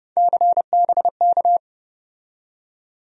Morse code, heard as a keyed tone, about 30 wpm, 700 Hz. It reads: CBK
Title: USA Call Sign Postfixes - 3 Letters 30wpm